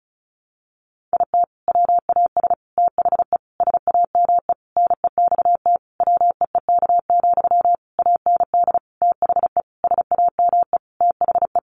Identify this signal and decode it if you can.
Morse code, heard as a keyed tone, about 34 wpm, 700 Hz. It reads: IT WAS THE SAME NEXT WEEK, AND THE SAME THE
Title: Sentences from Top 500 Words 35wpm